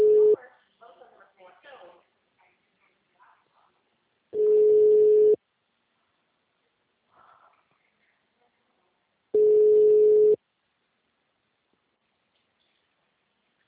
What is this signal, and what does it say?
Ringback tone (Europe, 425 Hz), with speech.